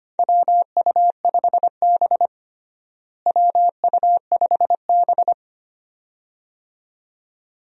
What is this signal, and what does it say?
Morse code, heard as a keyed tone, about 25 wpm, 700 Hz. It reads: WU5B WU5B
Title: USA Call Signs - 4 Characters - Repeated 2x - 25wpm